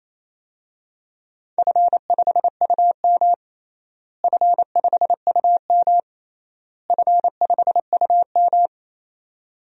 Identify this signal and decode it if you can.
Morse code, heard as a keyed tone, about 28 wpm, 700 Hz. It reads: F5UM F5UM F5UM
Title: DXCC-100 Call Signs - 4 Characters - Repeated 3x - 28wpm